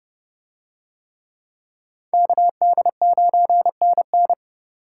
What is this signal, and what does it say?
Morse code, heard as a keyed tone, about 28 wpm, 700 Hz. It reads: KD9NN